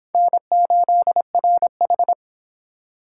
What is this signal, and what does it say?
Morse code, heard as a keyed tone, about 25 wpm, 700 Hz. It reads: N8RH